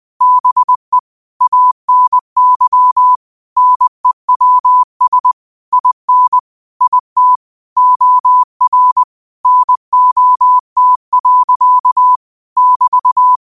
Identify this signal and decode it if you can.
Morse code, heard as a keyed tone, about 20 wpm, 1000 Hz. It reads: BE ANY NEWS IN IT OR NOT. <BT>
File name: qotdfiles / 20wpm